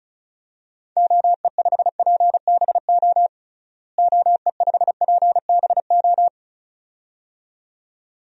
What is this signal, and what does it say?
Morse code, heard as a keyed tone, about 36 wpm, 700 Hz. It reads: OE5PBO OE5PBO
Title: DXCC-100 Call Signs - 6 Characters - Repeated 2x - 35wpm